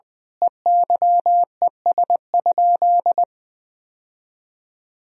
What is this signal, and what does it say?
Morse code, heard as a keyed tone, about 20 wpm, 700 Hz. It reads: EYES?
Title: Sentences from Top 1000 Words - Review Part 4 - 20wpm